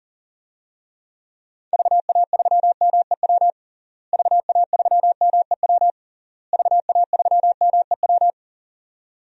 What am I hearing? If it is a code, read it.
Morse code, heard as a keyed tone, about 37 wpm, 700 Hz. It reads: VA3MEW VA3MEW VA3MEW
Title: DXCC-100 Call Signs - 6 Characters - Repeated 3x - 40wpm